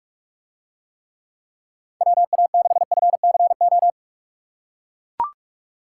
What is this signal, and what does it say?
Morse code, heard as a keyed tone, about 44 wpm, 700 Hz. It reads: WA6FCO